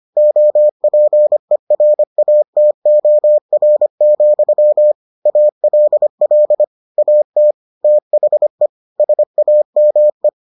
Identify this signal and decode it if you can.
Morse code, heard as a keyed tone, about 25 wpm, 600 Hz. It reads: OPERATOR, ALL AT THE SAME